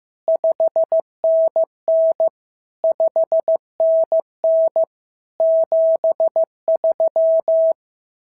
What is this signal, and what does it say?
Morse code, heard as a keyed tone, about 15 wpm, 650 Hz. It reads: 5NN 5NN 73
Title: SOTA QSO Practice 15wpm